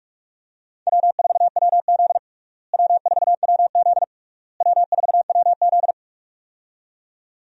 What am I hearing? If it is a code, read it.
Morse code, heard as a keyed tone, about 46 wpm, 700 Hz. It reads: W4WZ W4WZ W4WZ
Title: USA Call Signs - 4 Characters - Repeated 3x - 45wpm